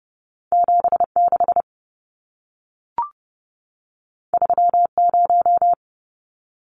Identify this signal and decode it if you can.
Morse code, heard as a keyed tone, about 30 wpm, 700 Hz. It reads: 76 30
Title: Numbers - 2-Digit 30wpm